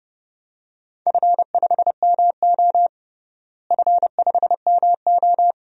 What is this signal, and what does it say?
Morse code, heard as a keyed tone, about 30 wpm, 700 Hz. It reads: F5MO F5MO
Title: DXCC-100 Call Signs - 4 Characters - Repeated 2x - 30wpm